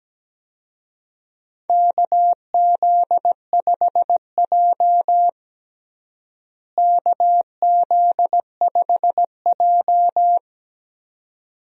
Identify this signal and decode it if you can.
Morse code, heard as a keyed tone, about 17 wpm, 700 Hz. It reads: KZ5J KZ5J
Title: USA Call Signs - 4 Characters - Repeated 2x - 17wpm